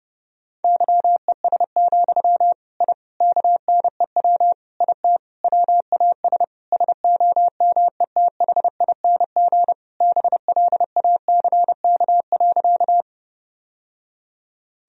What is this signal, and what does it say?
Morse code, heard as a keyed tone, about 30 wpm, 700 Hz. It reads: YES, I KNEW IT WAS SOMETHING BLACK.
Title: Sentences from Top 1000 Words - Review Part 2 - 30wpm